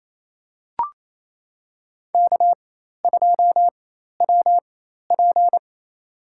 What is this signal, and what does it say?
Morse code, heard as a keyed tone, about 28 wpm, spaced as for 13 wpm, 700 Hz. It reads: K2WP
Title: USA Call Signs - 4 Characters - 4x Character Spacing - 28wpm